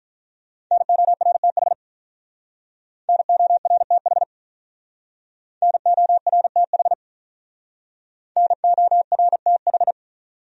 Morse code, heard as a keyed tone, about 46 wpm, 700 Hz. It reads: NORTH NORTH NORTH NORTH